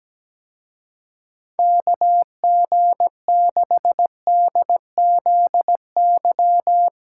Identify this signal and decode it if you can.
Morse code, heard as a keyed tone, about 20 wpm, 700 Hz. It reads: KG6DZY
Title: Speed Racing - USA 2x3 Call Signs 17wpm